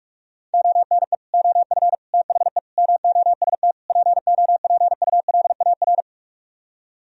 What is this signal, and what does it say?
Morse code, heard as a keyed tone, about 50 wpm, 700 Hz. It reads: ONE OF THE MOST POPULAR